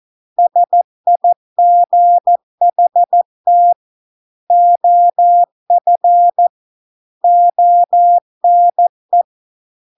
Morse code, heard as a keyed tone, about 14 wpm, 700 Hz. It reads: SIGHT OF ONE